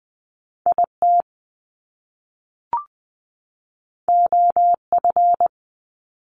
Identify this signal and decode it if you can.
Morse code, heard as a keyed tone, about 20 wpm, 700 Hz. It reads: I T O F
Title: Top Two Letter Combinations in Words 20wpm